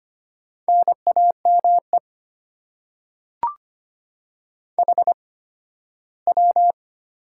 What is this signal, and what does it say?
Morse code, heard as a keyed tone, about 25 wpm, spaced as for 15 wpm, 700 Hz. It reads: NAME HW
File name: Top 90 QSO Elements - 8x Character Spacing - 25wpm